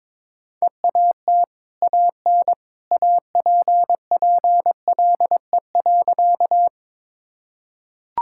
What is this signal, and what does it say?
Morse code, heard as a keyed tone, about 22 wpm, 690 Hz. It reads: EAT AN APPLE.
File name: practice / 22wpm / Easy Sentences with Hints - 22wpm